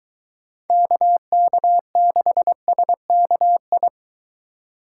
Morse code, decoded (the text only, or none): KK6SKI